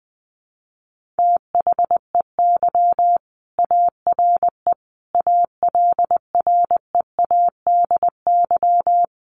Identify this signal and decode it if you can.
Morse code, heard as a keyed tone, about 20 wpm, 700 Hz. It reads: THEY ARE ALREADY
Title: Sentences from Top 1500 Words 20wpm